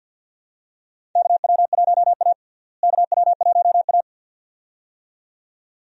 Morse code, heard as a keyed tone, about 52 wpm, 700 Hz. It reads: KW1A KW1A
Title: USA Call Signs - 4 Characters - Repeated 2x - 50wpm